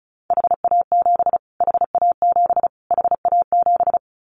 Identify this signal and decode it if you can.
Morse code, heard as a keyed tone, about 34 wpm, 700 Hz. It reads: HA7 HA7 HA7
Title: Callsign Prefixes for Top 25 DXCC - Repeated 3x - 35wpm